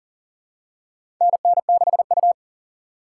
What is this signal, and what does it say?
Morse code, heard as a keyed tone, about 40 wpm, 700 Hz. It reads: NN6U
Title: USA Call Signs - 4 Characters - 2x Character Spacing - 40wpm